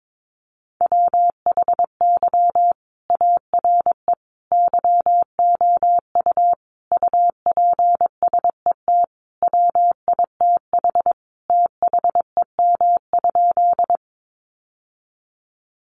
Morse code, heard as a keyed tone, about 22 wpm, 700 Hz. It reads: WHY ARE YOU UPSET WITH THEM?
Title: Sentences from Top 800 Words 22wpm